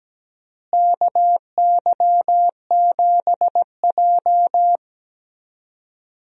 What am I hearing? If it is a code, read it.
Morse code, heard as a keyed tone, about 17 wpm, 700 Hz. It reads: KY7J
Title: USA Call Signs - 4 Characters - Repeated 3x - 17wpm